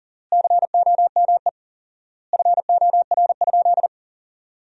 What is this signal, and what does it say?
Morse code, heard as a keyed tone, about 37 wpm, 700 Hz. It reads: COME FOR?
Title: 4x Word-Spacing - Sentences from Top 100 Words - 40wpm